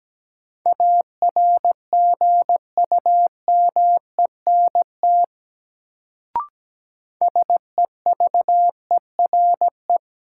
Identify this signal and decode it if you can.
Morse code, heard as a keyed tone, about 17 wpm, 700 Hz. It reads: ARGUMENT SEVERE
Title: Sentences from Top 1400 Words - New Word Drill - 17wpm